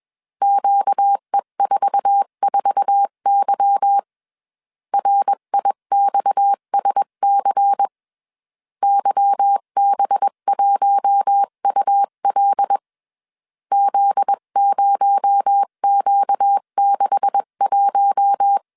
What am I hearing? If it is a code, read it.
Morse code, heard as a keyed tone, about 21 wpm, 800 Hz. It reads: QE44Y RIXSC YB1UL Z0Q61
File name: CW / DET / 1 9 1500C S B 20wpm 2313